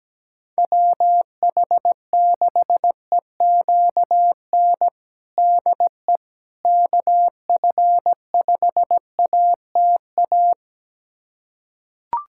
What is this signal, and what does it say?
Morse code, heard as a keyed tone, about 17 wpm, 700 Hz. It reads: WH6EQN DE KF5ATA
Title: USA Callsign DE Callsign - 6 Characters - 17wpm